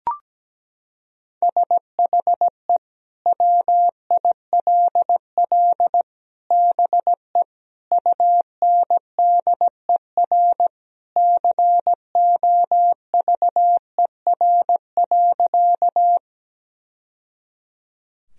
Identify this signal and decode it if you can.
Morse code, heard as a keyed tone, about 17 wpm, 700 Hz. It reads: SHE WILL BE UNDER COVER.
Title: Sentences from Top 1000 Words - Review Part 1 - 17wpm